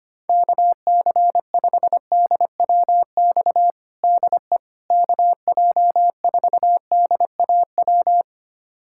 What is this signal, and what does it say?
Morse code, heard as a keyed tone, about 25 wpm, 700 Hz. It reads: KC5DWX DE KJ4DAW
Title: USA Callsign DE Callsign - 6 Characters - 25wpm